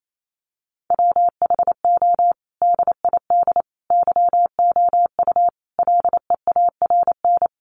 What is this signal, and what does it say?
Morse code, heard as a keyed tone, about 28 wpm, 700 Hz. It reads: WHO DID YOU LEARN
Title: Sentences from Top 300 Words 28wpm